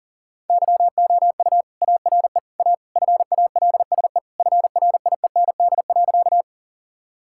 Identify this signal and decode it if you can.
Morse code, heard as a keyed tone, about 37 wpm, 700 Hz. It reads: YOU ARE A FALSE FRIEND.
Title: Sentences from Top 1600 Words 40wpm